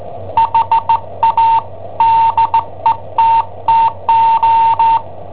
Morse code, heard as a keyed tone, about 16 wpm, 910 Hz. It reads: HA DETTO
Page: Cassetta esercizi preparata da I7OHP per IW5ELP